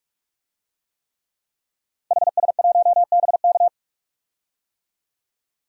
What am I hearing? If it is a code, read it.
Morse code, heard as a keyed tone, about 43 wpm, 700 Hz. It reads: HS1BK
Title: DXCC-100 Call Signs - 5 Characters - 45wpm